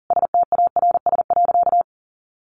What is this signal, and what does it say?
Morse code, heard as a keyed tone, about 40 wpm, 700 Hz. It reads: STARS.
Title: Additional Easy Phrases 40wpm